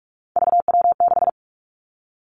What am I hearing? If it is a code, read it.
Morse code, heard as a keyed tone, about 46 wpm, 700 Hz. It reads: VW6
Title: Three Challenging Letter-Number Combinations 45wpm